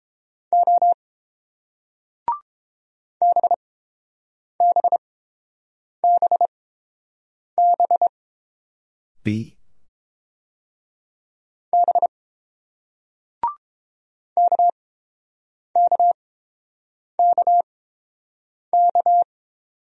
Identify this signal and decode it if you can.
Morse code, heard as a keyed tone, about 30 wpm, spaced as for 7 wpm, 700 Hz. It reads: O BBBB B KKKK